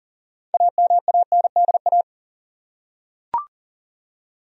Morse code, heard as a keyed tone, about 40 wpm, 700 Hz. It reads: AMANDA
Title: USA Common Names - 40wpm